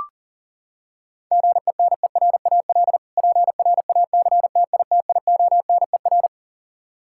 Morse code, heard as a keyed tone, about 37 wpm, 700 Hz. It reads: GENERAL PRACTITIONER
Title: Practice QSO - Exchange 3 - More Info - 40wpm